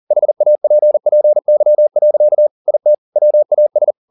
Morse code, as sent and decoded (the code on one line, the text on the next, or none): .... .- .--. .--. -.-- .-.-.- / .. - / .-- .- ...
HAPPY. IT WAS